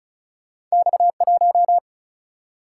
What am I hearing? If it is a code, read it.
Morse code, heard as a keyed tone, about 36 wpm, 700 Hz. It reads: X1
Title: Single-Letter Single-Number 35wpm